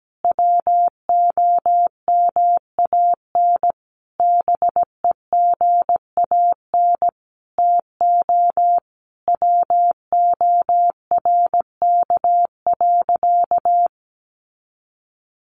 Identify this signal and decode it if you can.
Morse code, heard as a keyed tone, about 17 wpm, 700 Hz. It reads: WOMAN BEGAN TO WORK.
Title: Sentences from Top 1100 Words 17wpm